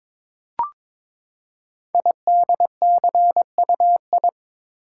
Morse code, heard as a keyed tone, about 22 wpm, 700 Hz. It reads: IDCUI
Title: Code Groups - 5 Letter:Numbers - 22wpm